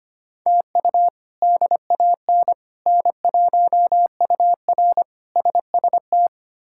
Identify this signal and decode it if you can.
Morse code, heard as a keyed tone, about 25 wpm, 700 Hz. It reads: TU DAN N1UR SST
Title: K1USN SST QSO Practice - 25wpm